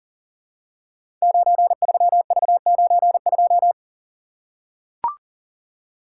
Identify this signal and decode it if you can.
Morse code, heard as a keyed tone, about 42 wpm, 690 Hz. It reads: 93V92
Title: Code Groups - 5 Letter:Numbers - 40wpm